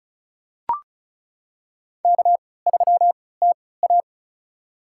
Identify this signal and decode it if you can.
Morse code, heard as a keyed tone, about 36 wpm, spaced as for 20 wpm, 700 Hz. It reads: K3TA
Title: USA Call Signs - 4 Characters - 3x Character Spacing - 35wpm